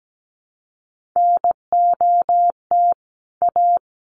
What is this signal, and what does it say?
Morse code, heard as a keyed tone, about 17 wpm, 700 Hz. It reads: NOT A
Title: Sets of 2 Words 17wpm